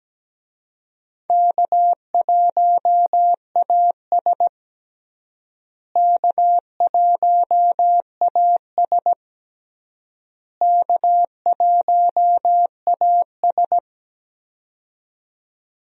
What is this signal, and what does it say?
Morse code, heard as a keyed tone, about 17 wpm, 700 Hz. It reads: K1AS K1AS K1AS
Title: USA Call Signs - 4 Characters - Repeated 3x - 17wpm